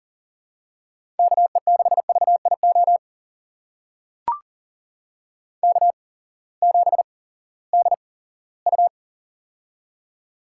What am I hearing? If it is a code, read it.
Morse code, heard as a keyed tone, about 40 wpm, spaced as for 20 wpm, 700 Hz. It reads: KE6VIO K7DU
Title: USA Call Signs - 8x Character Spacing - 40wpm